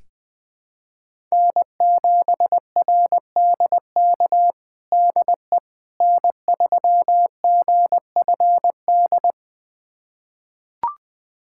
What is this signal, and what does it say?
Morse code, heard as a keyed tone, about 21 wpm, 700 Hz. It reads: N7RDK DE N3GFD
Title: USA Callsign DE Callsign - 5 Characters - 20wpm